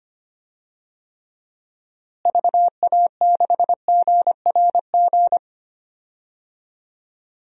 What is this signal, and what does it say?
Morse code, heard as a keyed tone, about 25 wpm, 700 Hz. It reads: VA6GRG
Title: DXCC-100 Call Signs - 6 Characters - Repeated 3x - 25wpm